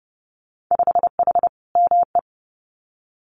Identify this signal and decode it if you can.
Morse code, heard as a keyed tone, about 30 wpm, 700 Hz. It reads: 5H ME
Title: Winter Field Day Exchanges Encore 30wpm